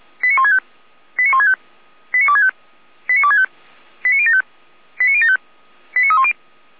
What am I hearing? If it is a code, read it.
Five-tone selective call (ZVEI), heard as quick runs of five tones, tones of about 70 ms. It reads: 89165 89165 89265 89265 89075 89075 89310